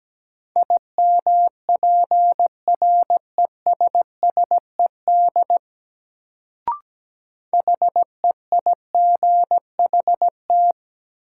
Morse code, heard as a keyed tone, about 17 wpm, 700 Hz. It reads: IMPRESSED HEIGHT
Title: Sentences from Top 1600 Words - New Word Drill Alt - 17wpm